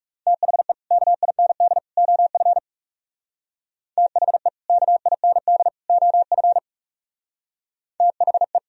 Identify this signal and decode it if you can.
Morse code, heard as a keyed tone, about 38 wpm, 700 Hz. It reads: THE KIND OF THE KIND OF THE